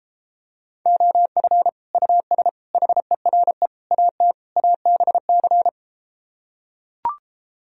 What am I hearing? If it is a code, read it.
Morse code, heard as a keyed tone, about 33 wpm, 700 Hz. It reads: OF US HERE AT ABC